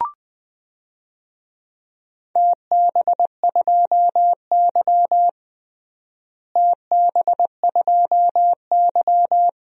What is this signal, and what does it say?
Morse code, heard as a keyed tone, about 20 wpm, 700 Hz. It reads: TB2Y TB2Y
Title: DXCC-100 Call Signs - 4 Characters - Repeated 2x - 20wpm